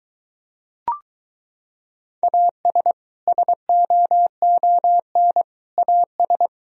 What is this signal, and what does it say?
Morse code, heard as a keyed tone, about 22 wpm, 700 Hz. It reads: AS SOON AS